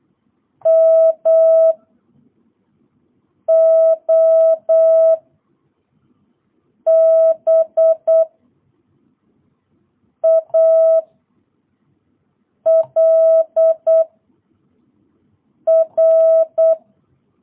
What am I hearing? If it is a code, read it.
Morse code, heard as a keyed tone, about 8 wpm, 660 Hz. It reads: M O B A L R